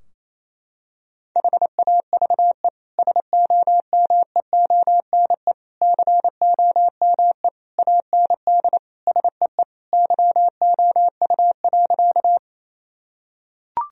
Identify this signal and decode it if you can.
Morse code, heard as a keyed tone, about 28 wpm, 700 Hz. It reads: HAVE SOMEONE COME AND SEE YOU.
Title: Sentences from Top 1000 Words - Review Part 4 - 28wpm